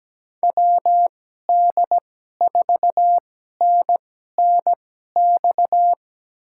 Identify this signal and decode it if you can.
Morse code, heard as a keyed tone, about 17 wpm, spaced as for 12 wpm, 700 Hz. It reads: WD4NNX
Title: USA Call Signs - 6 Characters - 2x Character Spacing - 17wpm